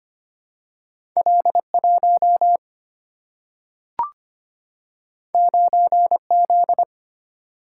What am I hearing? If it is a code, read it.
Morse code, heard as a keyed tone, about 25 wpm, 700 Hz. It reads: L1 9Z
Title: Single-Letter Single-Number 25wpm